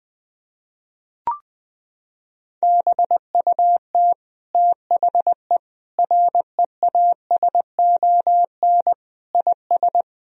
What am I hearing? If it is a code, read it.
Morse code, heard as a keyed tone, about 20 wpm, 700 Hz. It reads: BUT THE REASON IS
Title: Sentences from Top 1000 Words - Review Part 2 - 20wpm